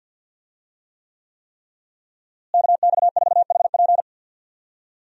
Morse code, heard as a keyed tone, about 47 wpm, 700 Hz. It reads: KX4HP
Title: USA Call Signs - 5 Characters - Repeated 3x - 50wpm